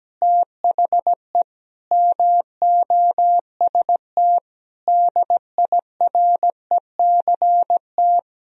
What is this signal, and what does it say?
Morse code, heard as a keyed tone, about 17 wpm, 700 Hz. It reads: THE MOST DIRECT
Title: Sentences from Top 1000 Words - Review Part 2 - 17wpm